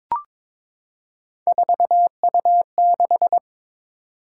Morse code, heard as a keyed tone, about 22 wpm, 700 Hz. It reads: 4U6
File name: practice / 22wpm / Three Letter-Number Combinations - 22wpm